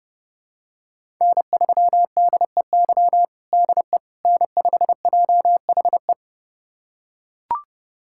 Morse code, heard as a keyed tone, about 30 wpm, 700 Hz. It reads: N3DEY DE N5JHE
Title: USA Callsign DE Callsign - 5 Characters - 30wpm